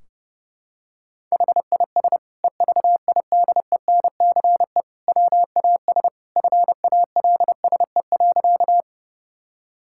Morse code, heard as a keyed tone, about 30 wpm, 700 Hz. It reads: HIS EVIDENCE WAS FALSE.
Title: Sentences from Top 1600 Words 30wpm